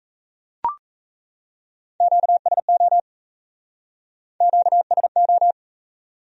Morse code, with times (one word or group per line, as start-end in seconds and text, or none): 2.0-3.0 QSO
4.4-5.5 QSO